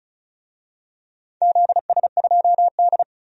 Morse code, heard as a keyed tone, about 34 wpm, 700 Hz. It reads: ZS2D
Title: DXCC-100 Call Signs - 4 Characters - 35wpm